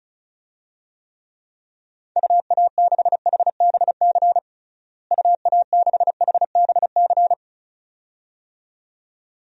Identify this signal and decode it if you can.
Morse code, heard as a keyed tone, about 34 wpm, 700 Hz. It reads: UA6HBC UA6HBC
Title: DXCC-100 Call Signs - 6 Characters - Repeated 2x - 35wpm